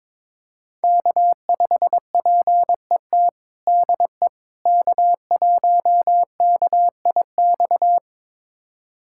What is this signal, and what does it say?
Morse code, heard as a keyed tone, about 22 wpm, 700 Hz. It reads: K5PET DE K1KIX
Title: USA Callsign DE Callsign - 5 Characters - 22wpm